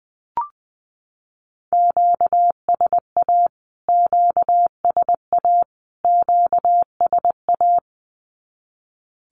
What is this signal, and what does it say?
Morse code, heard as a keyed tone, about 20 wpm, 700 Hz. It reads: QSA QSA QSA
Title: Q-Codes Repeated 3x 20wpm